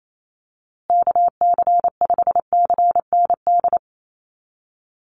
Morse code, heard as a keyed tone, about 28 wpm, 700 Hz. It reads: KC5CND
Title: USA Call Signs Repeated 2x 28wpm